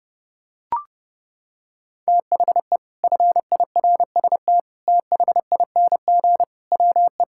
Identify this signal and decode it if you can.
Morse code, heard as a keyed tone, about 30 wpm, 700 Hz. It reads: THE FIRST THING WE
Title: Sentences from Top 1000 Words - Review Part 6 - 30wpm